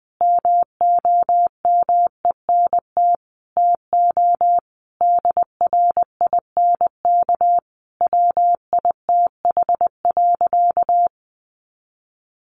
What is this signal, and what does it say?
Morse code, heard as a keyed tone, about 19 wpm, 700 Hz. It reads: MOMENT TO DRINK WITH.
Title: Sentences from Top 1500 Words 20wpm